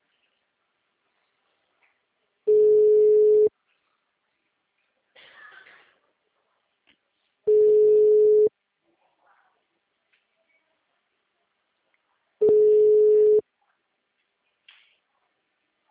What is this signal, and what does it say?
Ringback tone (Europe, 425 Hz).
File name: out- -2012- - - .36